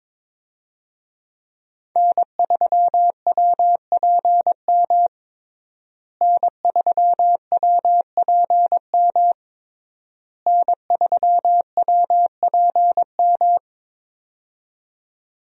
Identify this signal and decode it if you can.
Morse code, heard as a keyed tone, about 22 wpm, 700 Hz. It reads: N3WPM N3WPM N3WPM
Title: USA Call Signs - 5 Characters - Repeated 3x - 22wpm